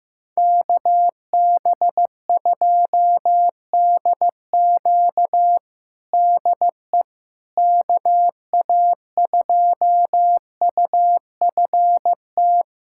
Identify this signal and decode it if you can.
Morse code, heard as a keyed tone, about 15 wpm, 700 Hz. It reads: KB2DQ DE KA2UFT
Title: USA Callsign DE Callsign 15wpm